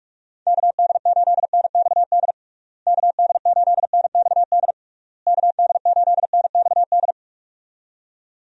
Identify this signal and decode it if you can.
Morse code, heard as a keyed tone, about 46 wpm, 700 Hz. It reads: KD8NXD KD8NXD KD8NXD
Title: USA Call Signs - 6 Characters - Repeated 3x - 45wpm